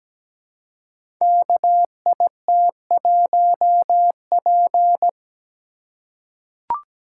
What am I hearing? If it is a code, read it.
Morse code, heard as a keyed tone, about 17 wpm, 700 Hz. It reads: KIT1P
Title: Code Groups - 5 Letter:Numbers - 17wpm